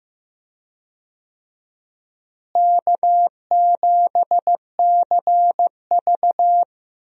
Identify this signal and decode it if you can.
Morse code, heard as a keyed tone, about 15 wpm, 700 Hz. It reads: K7CV
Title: USA Call Signs - 4 Characters - Repeated 3x - 15wpm